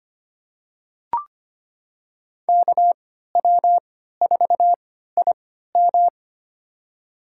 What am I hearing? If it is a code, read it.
Morse code, heard as a keyed tone, about 25 wpm, spaced as for 14 wpm, 700 Hz. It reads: KW4IM
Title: USA Call Signs - 3x Character Spacing - 25wpm